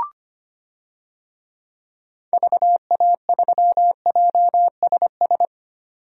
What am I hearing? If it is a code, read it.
Morse code, heard as a keyed tone, about 25 wpm, 690 Hz. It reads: VA3JSS
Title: DXCC-100 Call Signs - 6 Characters - 25wpm